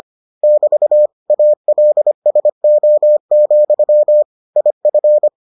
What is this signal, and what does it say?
Morse code, heard as a keyed tone, about 25 wpm, 600 Hz. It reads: <BT> ALSO, IF